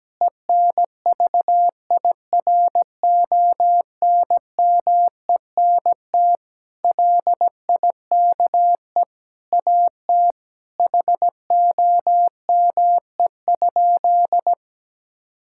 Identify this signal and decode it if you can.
Morse code, heard as a keyed tone, about 17 wpm, 700 Hz. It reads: ENVIRONMENT LIKE AT HOME?
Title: Sentences from Top 1000 Words - Review Part 5 - 17wpm